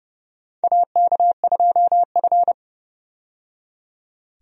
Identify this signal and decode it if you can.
Morse code, heard as a keyed tone, about 30 wpm, 700 Hz. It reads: AK2F
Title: USA Call Signs - 4 Characters - 30wpm